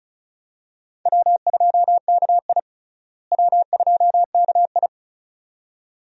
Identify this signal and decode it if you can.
Morse code, heard as a keyed tone, about 36 wpm, 700 Hz. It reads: W2KI W2KI
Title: USA Call Signs - 4 Characters - Repeated 2x - 35wpm